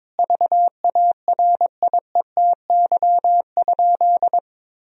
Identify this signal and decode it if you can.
Morse code, heard as a keyed tone, about 22 wpm, 700 Hz. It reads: VARIETY?
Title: Sentences from Top 1000 Words - Review Part 4 - 22wpm